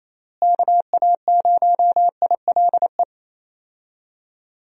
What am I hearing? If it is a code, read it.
Morse code, heard as a keyed tone, about 27 wpm, 700 Hz. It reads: KA0ILE
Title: USA Call Signs - 6 Characters - Repeated 2x - 28wpm